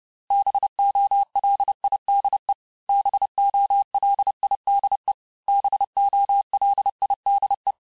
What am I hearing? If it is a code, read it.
Morse code, heard as a keyed tone, about 30 wpm, 800 Hz. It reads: BOLIDE BOLIDE BOLIDE
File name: cours-du-mercredi / audios / cours-854-mots-vitesse-28-mpm